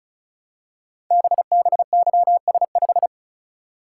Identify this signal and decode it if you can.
Morse code, heard as a keyed tone, about 36 wpm, 700 Hz. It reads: BBYS5
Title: Code Groups - 5 Letter:Numbers - 35wpm